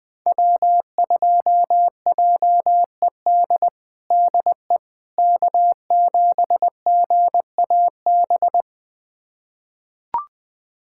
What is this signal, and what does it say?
Morse code, heard as a keyed tone, about 20 wpm, 700 Hz. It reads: W2JED DE K7GAB
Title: USA Callsign DE Callsign - 5 Characters - 20wpm